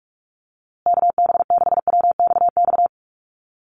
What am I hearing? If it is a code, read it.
Morse code, heard as a keyed tone, about 44 wpm, 690 Hz. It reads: KB6WXX
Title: USA Call Signs Repeated 2x 45wpm